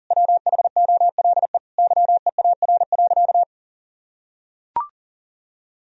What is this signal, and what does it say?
Morse code, heard as a keyed tone, about 40 wpm, 700 Hz. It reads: WHOLE YEAR.
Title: Sentences from Top 1000 Words - Review Part 1 - 40wpm